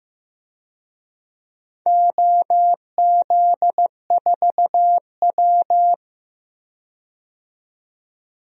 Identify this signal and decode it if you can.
Morse code, heard as a keyed tone, about 15 wpm, 700 Hz. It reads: OZ4W
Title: DXCC-100 Call Signs - 4 Characters - 15wpm